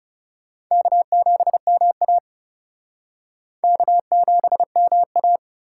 Morse code, heard as a keyed tone, about 35 wpm, 700 Hz. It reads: K7MA K7MA
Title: Speed Racing - USA Call Signs 30wpm